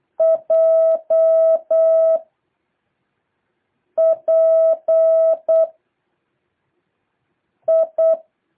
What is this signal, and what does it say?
Morse code, heard as a keyed tone, about 8 wpm, 650 Hz. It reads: J P I
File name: jh8cbh / 2023cw